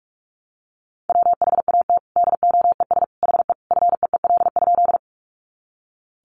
Morse code, heard as a keyed tone, about 44 wpm, 700 Hz. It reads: WHAT DOES HE FEEL?
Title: Sentences from Top 400 Words 45wpm